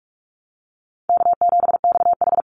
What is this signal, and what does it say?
Morse code, heard as a keyed tone, about 44 wpm, 700 Hz. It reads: K7XH
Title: Call Signs - 45wpm